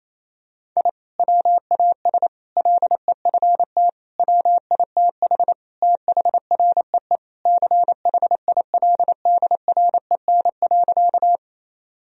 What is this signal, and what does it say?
Morse code, heard as a keyed tone, about 28 wpm, 700 Hz. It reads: I WAS LEFT WITH THREE CHILDREN.
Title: Sentences from Top 1000 Words - Review Part 1 - 28wpm